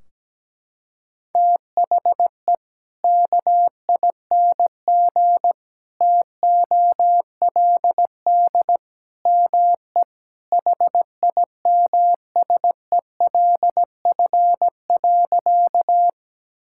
Morse code, heard as a keyed tone, about 17 wpm, 700 Hz. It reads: THE KING TOLD ME HIMSELF.
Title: Sentences from Top 1600 Words 17wpm